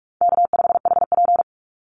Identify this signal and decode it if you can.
Morse code, heard as a keyed tone, about 44 wpm, 700 Hz. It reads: K5HP
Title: Easy Call Signs - 45wpm